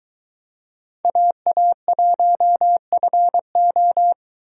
Morse code, heard as a keyed tone, about 21 wpm, 700 Hz. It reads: AA1FO